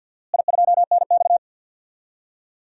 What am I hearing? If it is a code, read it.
Morse code, heard as a keyed tone, about 52 wpm, 700 Hz. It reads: I2NX
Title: DXCC-100 Call Signs - 4 Characters - Repeated 2x - 50wpm